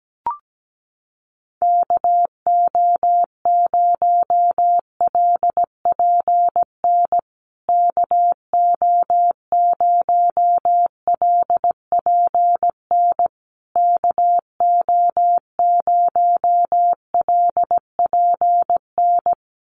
Morse code, heard as a keyed tone, about 17 wpm, 700 Hz. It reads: KO0LPN KO0LPN KO0LPN
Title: USA Call Signs Repeated 3x 17wpm